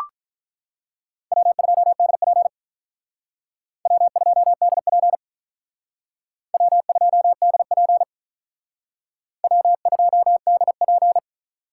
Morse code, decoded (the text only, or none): W2DP W2DP W2DP W2DP